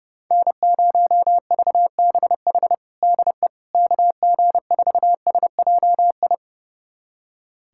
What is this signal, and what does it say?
Morse code, heard as a keyed tone, about 30 wpm, 700 Hz. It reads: N0VBH DE KG4SJI
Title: USA Callsign DE Callsign 30wpm